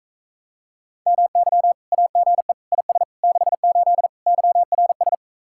Morse code, heard as a keyed tone, about 44 wpm, 700 Hz. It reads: MY AGE IS 68 YRS